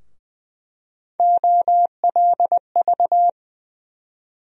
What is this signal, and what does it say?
Morse code, heard as a keyed tone, about 20 wpm, 700 Hz. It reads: OLV
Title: USA Call Sign Postfixes - 3 Letters 20wpm